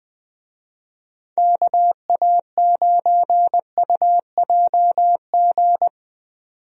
Morse code, heard as a keyed tone, about 20 wpm, 700 Hz. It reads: KA9UJG
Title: USA Call Signs - 6 Characters - 20wpm